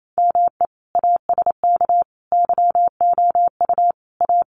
Morse code, heard as a keyed tone, about 28 wpm, 700 Hz. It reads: ME ASK YOU A
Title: Sets of 5 Words Encore2 28wpm